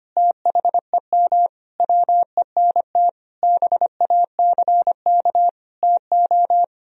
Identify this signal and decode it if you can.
Morse code, heard as a keyed tone, about 25 wpm, 700 Hz. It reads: THEM WENT BACK TO
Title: Sentences from Top 1000 Words - Review Part 1 - 25wpm